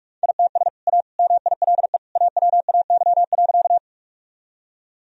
Morse code, heard as a keyed tone, about 41 wpm, 700 Hz. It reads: ITS A MILE AWAY.
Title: Sentences from Top 1000 Words - Review Part 1 - 45wpm